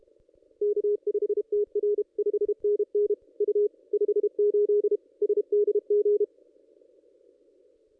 Morse code, heard as a keyed tone, about 32 wpm, 410 Hz. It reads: K5TR 5NN U 58 SDG
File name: contests / ss-cw-k5tr-op-n5rz -1800